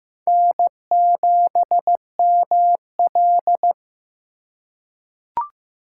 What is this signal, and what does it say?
Morse code, heard as a keyed tone, about 16 wpm, 700 Hz. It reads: N7ML
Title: K1USN SST QSO Practice - 15wpm